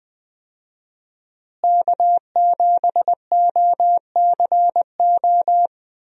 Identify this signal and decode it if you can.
Morse code, heard as a keyed tone, about 20 wpm, 700 Hz. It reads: K7OCO
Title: USA Call Signs - 5 Characters - Repeated 3x - 20wpm